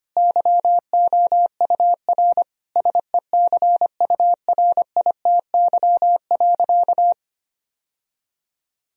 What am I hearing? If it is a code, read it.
Morse code, heard as a keyed tone, about 25 wpm, 700 Hz. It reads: YOUR SECURITY.
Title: Sentences from Top 1000 Words - Review Part 4 - 25wpm